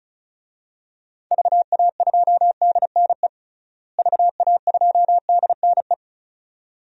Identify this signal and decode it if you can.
Morse code, heard as a keyed tone, about 36 wpm, 700 Hz. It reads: VA2DNE VA2DNE
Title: DXCC-100 Call Signs - 6 Characters - Repeated 2x - 35wpm